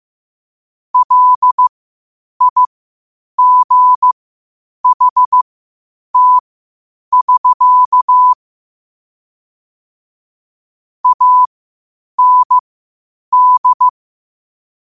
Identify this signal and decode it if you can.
Morse code, heard as a keyed tone, about 15 wpm, spaced as for 8 wpm, 1000 Hz. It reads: LIGHT <SK> AND
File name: qotdfiles / 8wpm